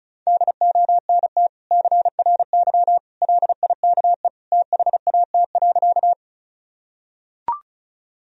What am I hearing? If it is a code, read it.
Morse code, heard as a keyed tone, about 34 wpm, 700 Hz. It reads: DONT CRY LIKE THAT.
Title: Sentences from Top 1000 Words - Review Part 2 - 35wpm